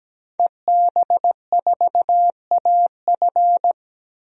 Morse code, heard as a keyed tone, about 17 wpm, 700 Hz. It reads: EB4AF
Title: DXCC-100 Call Signs - 5 Characters - 17wpm